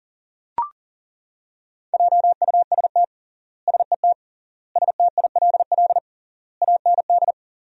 Morse code, heard as a keyed tone, about 40 wpm, 700 Hz. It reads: JUST SET STILL AND
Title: 3x Word-Spacing - Sentences from Top 300 Words - 40wpm